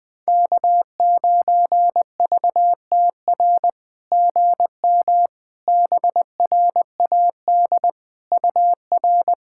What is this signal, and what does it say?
Morse code, heard as a keyed tone, about 20 wpm, 700 Hz. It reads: K9VTR GM BRAD UR
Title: SOTA QSO Practice 20wpm